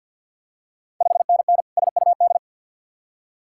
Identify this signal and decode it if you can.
Morse code, heard as a keyed tone, about 51 wpm, 690 Hz. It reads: 5NN SUD
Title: Ontario QSO Party Exchange 50wpm